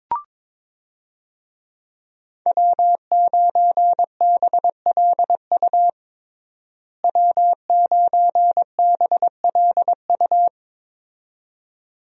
Morse code, heard as a keyed tone, about 22 wpm, 700 Hz. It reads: W9BLU W9BLU
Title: USA Call Signs - 5 Characters - Repeated 2x - 22wpm